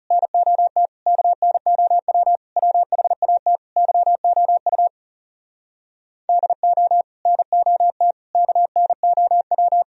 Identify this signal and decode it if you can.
Morse code, heard as a keyed tone, about 37 wpm, 700 Hz. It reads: NOT KNOW WHAT YOU DO NOT KNOW
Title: Speed-Racing - Sets of 5 Words 30wpm